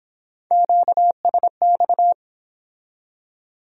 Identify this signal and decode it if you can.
Morse code, heard as a keyed tone, about 24 wpm, 700 Hz. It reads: QSX